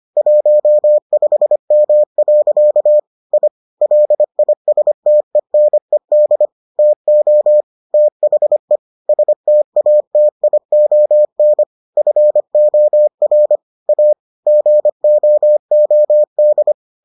Morse code, heard as a keyed tone, about 25 wpm, 600 Hz. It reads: 15M. I LISTENED TO THE STATION FOR A GOOD